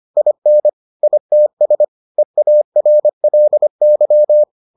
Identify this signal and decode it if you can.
Morse code, heard as a keyed tone, about 25 wpm, 600 Hz. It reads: IN ITS EARLY